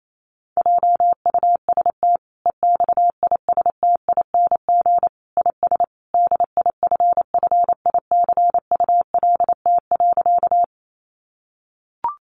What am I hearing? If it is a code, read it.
Morse code, heard as a keyed tone, about 28 wpm, 700 Hz. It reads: JUST EXISTING IS DIFFICULT.
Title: Sentences from Top 1500 Words 28wpm